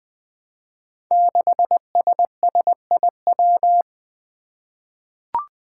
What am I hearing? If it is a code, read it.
Morse code, heard as a keyed tone, about 20 wpm, 700 Hz. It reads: 6SSIW
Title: Code Groups - 5 Letter:Numbers - 20wpm